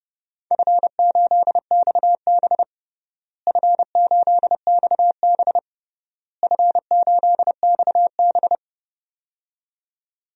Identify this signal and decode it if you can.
Morse code, heard as a keyed tone, about 29 wpm, 700 Hz. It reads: F8XB F8XB F8XB
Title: DXCC-100 Call Signs - 4 Characters - Repeated 3x - 30wpm